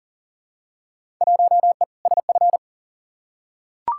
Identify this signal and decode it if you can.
Morse code, heard as a keyed tone, about 40 wpm, 700 Hz. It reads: 1E SF
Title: Field Day Exchanges - 40wpm